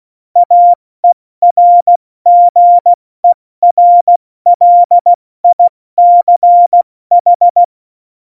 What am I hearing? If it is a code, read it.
Morse code, heard as a keyed tone, about 16 wpm, spaced as for 14 wpm, 700 Hz. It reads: AERGERLICH
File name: Morsekurs / Witze / Witz 15. Oktober 2019, 70 F80